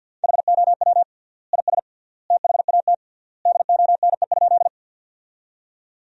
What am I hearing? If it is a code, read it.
Morse code, heard as a keyed tone, about 49 wpm, 700 Hz. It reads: HOW IS THAT DONE?
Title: 3x Word-Spacing - Sentences from Top 500 Words - 50wpm